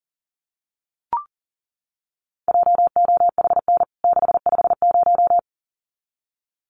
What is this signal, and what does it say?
Morse code, heard as a keyed tone, about 40 wpm, 700 Hz. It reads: JOHN 650
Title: CWT Exchange 40wpm